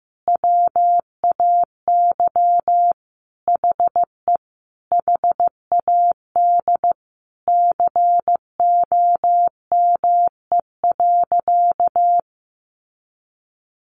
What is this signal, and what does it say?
Morse code, heard as a keyed tone, about 15 wpm, 700 Hz. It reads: WAY HE HAD COME.
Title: Sentences from Top 1100 Words 15wpm